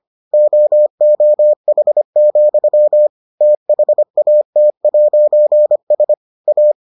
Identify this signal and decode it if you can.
Morse code, heard as a keyed tone, about 25 wpm, 600 Hz. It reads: OOH, THAT'S A